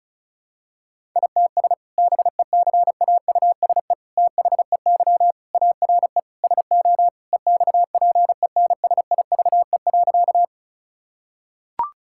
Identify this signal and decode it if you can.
Morse code, heard as a keyed tone, about 36 wpm, 700 Hz. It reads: ITS BECAUSE THEY ARE SO EXPENSIVE.
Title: Sentences from Top 1000 Words - Review Part 6 - 35wpm